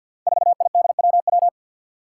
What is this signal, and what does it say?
Morse code, heard as a keyed tone, about 47 wpm, 700 Hz. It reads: 4IDWW
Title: Five Challenging Letter-Number Combinations 50wpm